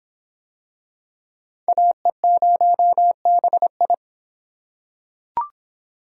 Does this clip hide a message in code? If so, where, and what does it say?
Yes: Morse code from 1.7 to 3.9 s, heard as a keyed tone, about 26 wpm, 700 Hz. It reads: AE0BI